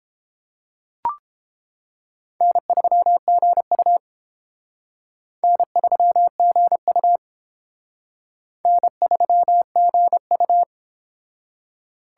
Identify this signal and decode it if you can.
Morse code, heard as a keyed tone, about 30 wpm, 700 Hz. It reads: N3GU N3GU N3GU